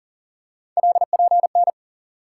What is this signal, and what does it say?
Morse code, heard as a keyed tone, about 40 wpm, 700 Hz. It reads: LPN
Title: K1USN SST QSO Practice 40wpm